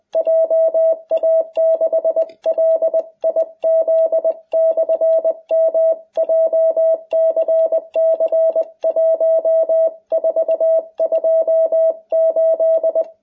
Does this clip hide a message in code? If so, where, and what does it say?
Yes: Morse code from 0.2 to 13.0 s, heard as a keyed tone, about 20 wpm, 660 Hz. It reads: JA6LIZ/M JCC 1428